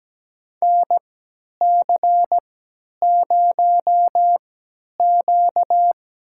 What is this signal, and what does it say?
Morse code, heard as a keyed tone, about 17 wpm, spaced as for 10 wpm, 700 Hz. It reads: NC0Q
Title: USA Call Signs - 3x Character Spacing - 17wpm